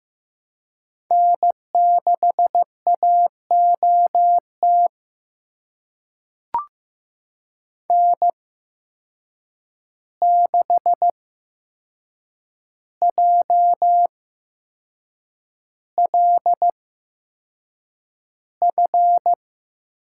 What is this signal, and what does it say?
Morse code, heard as a keyed tone, about 15 wpm, spaced as for 6 wpm, 700 Hz. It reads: N6AOT N6JLF
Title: USA Call Signs - 5 Characters - 8x Character Spacing - 15wpm